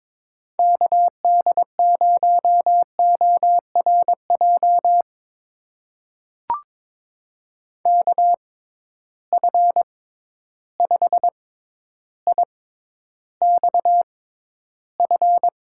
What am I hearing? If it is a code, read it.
Morse code, heard as a keyed tone, about 22 wpm, spaced as for 11 wpm, 700 Hz. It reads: KD0ORJ KF5IXF
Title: USA Call Signs - 6x Character Spacing - 22wpm